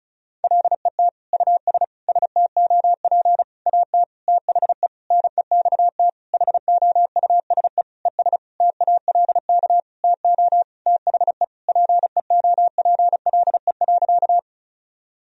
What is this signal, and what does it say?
Morse code, heard as a keyed tone, about 36 wpm, 700 Hz. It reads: LET US STOP AT THE NEXT HOUSE ES TALK TO THE PEOPLE.
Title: Sentences from Top 400 Words and Abbreviations 35wpm